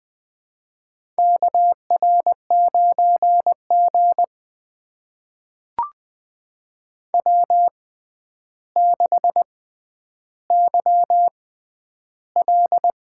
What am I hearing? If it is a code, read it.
Morse code, heard as a keyed tone, about 20 wpm, spaced as for 10 wpm, 700 Hz. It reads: KR9G W6YL
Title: USA Call Signs - 4 Characters - 6x Character Spacing - 20wpm